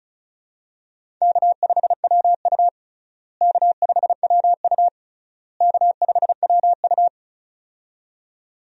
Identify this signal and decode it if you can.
Morse code, heard as a keyed tone, about 36 wpm, 700 Hz. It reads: K5WU K5WU K5WU
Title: USA Call Signs - 4 Characters - Repeated 3x - 35wpm